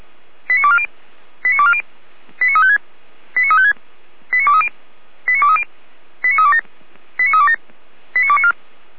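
Five-tone selective call (ZVEI), heard as quick runs of five tones, tones of about 70 ms. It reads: 89240 89240 89356 89356 89230 89230 89327 89327 89294
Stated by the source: ZVEI five-tone call 89115 received by SDR.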